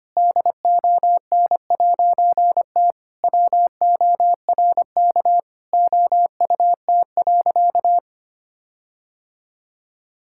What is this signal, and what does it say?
Morse code, heard as a keyed tone, about 25 wpm, 700 Hz. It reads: DON'T WORK OUT.
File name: practice / 25wpm / All-Age Appropriate Jokes - 25wpm